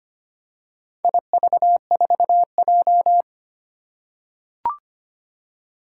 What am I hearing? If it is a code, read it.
Morse code, heard as a keyed tone, about 25 wpm, 700 Hz. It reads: IV4J
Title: Four Challenging Letter-Number Combinations 25wpm